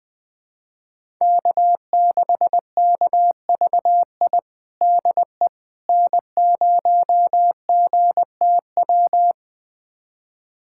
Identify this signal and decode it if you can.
Morse code, heard as a keyed tone, about 20 wpm, 700 Hz. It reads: K6KVI DE N0GTW
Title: USA Callsign DE Callsign - 5 Characters - 20wpm